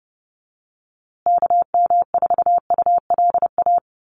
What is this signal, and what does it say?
Morse code, heard as a keyed tone, about 30 wpm, 700 Hz. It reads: KM4ULA
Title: USA Call Signs Repeated 2x 30wpm